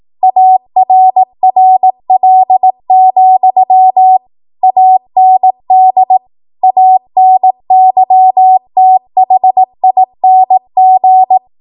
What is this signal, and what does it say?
Morse code, heard as a keyed tone, about 18 wpm, 750 Hz. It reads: ARRL, AND ANYTHING
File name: Archive / 18 WPM / 18WPM